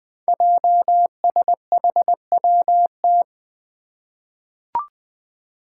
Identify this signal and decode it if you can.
Morse code, heard as a keyed tone, about 20 wpm, 700 Hz. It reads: JSHWT
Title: Five Challenging Letter-Number Combinations 20wpm